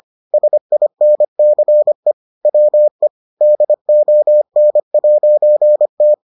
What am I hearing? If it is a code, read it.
Morse code, heard as a keyed tone, about 25 wpm, 600 Hz. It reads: SINCE WE DON'T